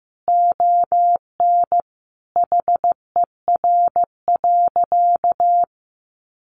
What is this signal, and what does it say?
Morse code, heard as a keyed tone, about 15 wpm, 700 Hz. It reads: ON HER.
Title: Sentences from Top 1300 Words 15wpm